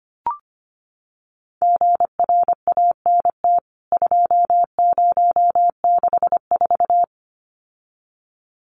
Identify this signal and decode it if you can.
Morse code, heard as a keyed tone, about 25 wpm, 700 Hz. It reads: GRANT 2064
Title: CWT Exchange 25wpm